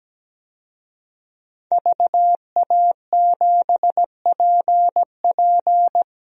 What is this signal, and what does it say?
Morse code, heard as a keyed tone, about 17 wpm, 700 Hz. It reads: VA7PP
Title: DXCC-100 Call Signs - 5 Characters - Repeated 3x - 17wpm